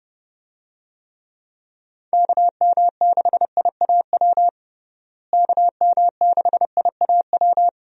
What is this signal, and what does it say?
Morse code, heard as a keyed tone, about 31 wpm, 700 Hz. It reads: KM6IAW KM6IAW
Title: USA Call Signs - 6 Characters - Repeated 2x - 30wpm